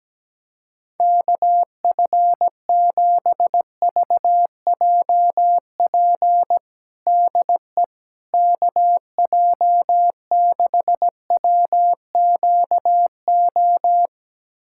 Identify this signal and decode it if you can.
Morse code, heard as a keyed tone, about 17 wpm, 700 Hz. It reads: KF7VJP DE KJ6WQO
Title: USA Callsign DE Callsign 17wpm